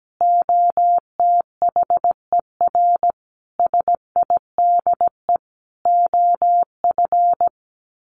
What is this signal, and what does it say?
Morse code, heard as a keyed tone, about 17 wpm, 700 Hz. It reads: OTHER SIDE OF
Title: Sets of 5 Words Encore 17wpm